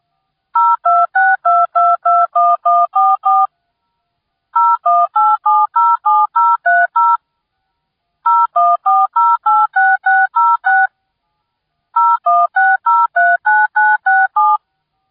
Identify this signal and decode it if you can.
DTMF, heard as telephone touch tones, tones of about 220 ms. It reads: 0262221144 018707030 014086606 016039967